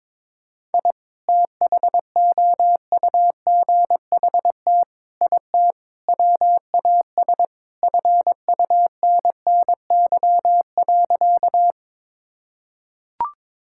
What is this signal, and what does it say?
Morse code, heard as a keyed tone, about 21 wpm, 700 Hz. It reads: I THOUGHT IT WAS FUNNY.
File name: practice / 22wpm / Sentences with 4-Word Phrases - 22wpm